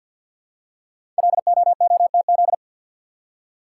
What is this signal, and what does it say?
Morse code, heard as a keyed tone, about 48 wpm, 700 Hz. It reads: LOOTZ
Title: Code Groups - 5 Letters - 50wpm